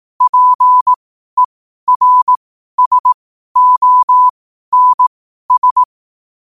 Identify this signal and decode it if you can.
Morse code, heard as a keyed tone, about 18 wpm, spaced as for 12 wpm, 1000 Hz. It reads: PERSONS
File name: qotdfiles / 12wpm